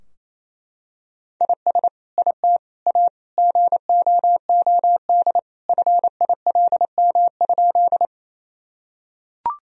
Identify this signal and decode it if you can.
Morse code, heard as a keyed tone, about 28 wpm, 700 Hz. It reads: IS IT A GOOD FILM?
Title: Sentences from Top 1000 Words - Review Part 3 - 28wpm